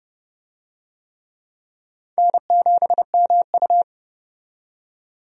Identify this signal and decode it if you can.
Morse code, heard as a keyed tone, about 30 wpm, 700 Hz. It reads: N7MU
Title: USA Call Signs - 4 Characters - 30wpm